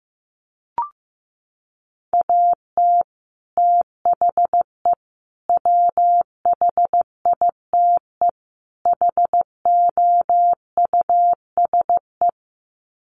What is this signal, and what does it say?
Morse code, heard as a keyed tone, about 15 wpm, 700 Hz. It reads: AT THE WHITE HOUSE
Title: Sets of 4 Words 15wpm